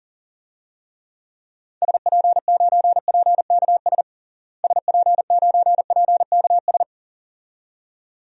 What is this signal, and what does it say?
Morse code, heard as a keyed tone, about 40 wpm, 700 Hz. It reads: SP9PKS SP9PKS
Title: DXCC-100 Call Signs - 6 Characters - Repeated 2x - 40wpm